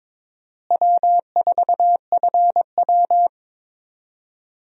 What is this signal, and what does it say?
Morse code, heard as a keyed tone, about 21 wpm, 700 Hz. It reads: W4FW
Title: USA Call Signs - 4 Characters - Repeated 2x - 22wpm